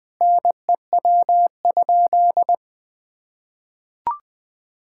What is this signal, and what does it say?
Morse code, heard as a keyed tone, about 20 wpm, 700 Hz. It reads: NEW?
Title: Sentences with 6-Word Phrases - Spoken Beginning - 20wpm